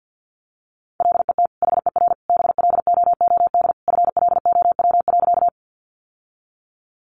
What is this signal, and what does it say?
Morse code, heard as a keyed tone, about 50 wpm, 700 Hz. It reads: LET HER BLOOD FLOW.
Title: Sentences from Top 1200 Words 50wpm